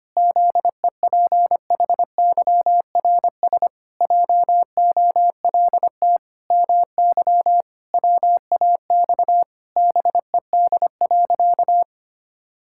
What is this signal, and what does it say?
Morse code, heard as a keyed tone, about 25 wpm, 700 Hz. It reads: ZEPHYRS JOLT MY WAX BED.
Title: Pangram Challenge v2 - 25wpm